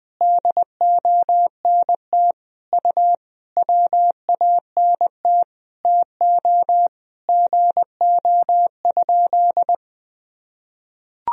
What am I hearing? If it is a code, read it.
Morse code, heard as a keyed tone, about 20 wpm, 700 Hz. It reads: DONT U WANT TO GO?
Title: Sentences from Top 300 Words and Abbreviations 20wpm